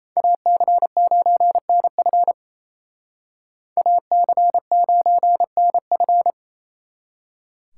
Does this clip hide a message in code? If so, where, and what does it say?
Yes: Morse code from 0.2 to 6.3 s, heard as a keyed tone, about 36 wpm, 700 Hz. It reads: AC9NF AC9NF